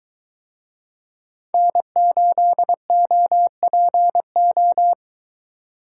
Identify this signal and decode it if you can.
Morse code, heard as a keyed tone, about 21 wpm, 700 Hz. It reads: N8OPO